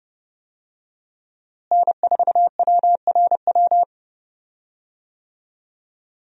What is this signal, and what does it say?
Morse code, heard as a keyed tone, about 32 wpm, 700 Hz. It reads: N4WRW
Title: USA Call Signs - 5 Characters - Repeated 3x - 30wpm